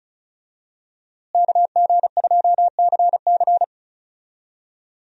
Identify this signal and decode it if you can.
Morse code, heard as a keyed tone, about 36 wpm, 700 Hz. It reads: KG2CC
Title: USA Call Signs - 5 Characters - Repeated 2x - 35wpm